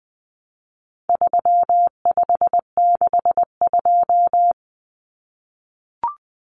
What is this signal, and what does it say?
Morse code, heard as a keyed tone, about 20 wpm, 700 Hz. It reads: 3562
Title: Numbers - 4-Digit 20wpm